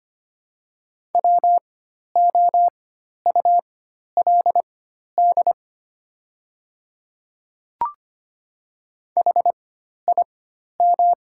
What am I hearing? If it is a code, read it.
Morse code, heard as a keyed tone, about 25 wpm, spaced as for 12 wpm, 700 Hz. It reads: WOULD HIM
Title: 4x Character Spacing - Top 200 Words - 25wpm